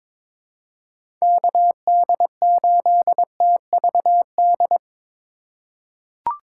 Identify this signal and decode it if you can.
Morse code, heard as a keyed tone, about 22 wpm, 700 Hz. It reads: KD8TVD
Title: USA Call Signs - 6 Characters - 2x Character Spacing - 22wpm